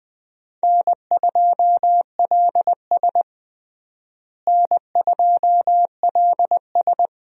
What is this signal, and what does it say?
Morse code, heard as a keyed tone, about 21 wpm, 700 Hz. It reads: N2LS N2LS
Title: USA Call Signs - 4 Characters - Repeated 2x - 20wpm